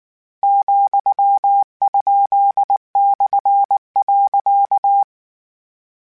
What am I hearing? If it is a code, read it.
Morse code, heard as a keyed tone, about 19 wpm, 800 Hz. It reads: ,?/.
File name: Misc / Misc-mp3s / Punct @18